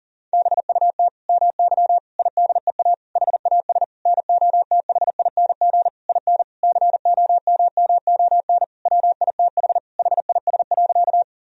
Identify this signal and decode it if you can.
Morse code, heard as a keyed tone, about 37 wpm, 700 Hz. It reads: BUT MY IDEA HAS NOTHING IN COMMON WITH HIS.
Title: Sentences from Top 1000 Words - Review Part 3 - 40wpm